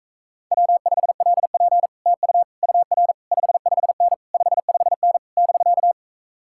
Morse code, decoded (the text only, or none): W5LP TU UR 55N 55N <BK>